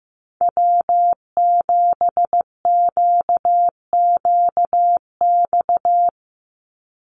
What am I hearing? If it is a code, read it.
Morse code, heard as a keyed tone, about 15 wpm, 700 Hz. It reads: W7QQX
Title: USA Call Signs 15wpm